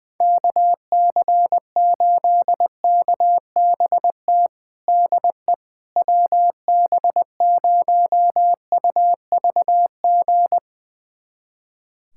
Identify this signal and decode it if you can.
Morse code, heard as a keyed tone, about 20 wpm, 700 Hz. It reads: KC8KBT DE WB0UVG
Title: USA Callsign DE Callsign - 6 Characters - 20wpm